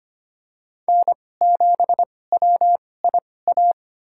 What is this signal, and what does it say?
Morse code, heard as a keyed tone, about 25 wpm, spaced as for 18 wpm, 700 Hz. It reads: N7WIA
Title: USA Call Signs - 5 Characters - 2x Character Spacing - 25wpm